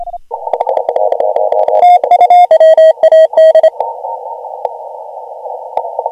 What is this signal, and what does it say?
Morse code, heard as a keyed tone, about 28 wpm, 1900 Hz, with noise. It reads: WAD
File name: AUDIO2019 / HG1G CW